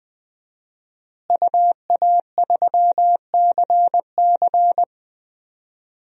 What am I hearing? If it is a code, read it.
Morse code, heard as a keyed tone, about 20 wpm, 700 Hz. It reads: UA3CC
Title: DXCC-100 Call Signs - 5 Characters - Repeated 2x - 20wpm